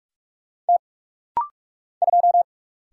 Morse code, heard as a keyed tone, about 44 wpm, 700 Hz. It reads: T 2
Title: Single Letter-Number - Rapid-Fire 45wpm